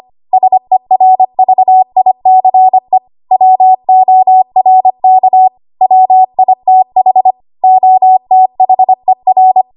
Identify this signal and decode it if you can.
Morse code, heard as a keyed tone, about 25 wpm, 750 Hz. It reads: SERVICE WORK WITH OTHER
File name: Archive / 25 WPM / 25WPM